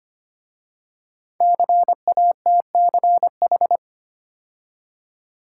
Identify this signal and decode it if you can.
Morse code, heard as a keyed tone, about 25 wpm, 700 Hz. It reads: CATCH
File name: 5 Letter Words - First 3 Letters Given - 25wpm